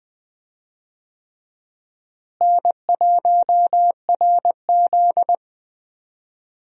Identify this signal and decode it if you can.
Morse code, heard as a keyed tone, about 21 wpm, 700 Hz. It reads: N1RZ
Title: USA Call Signs - 4 Characters - Repeated 3x - 20wpm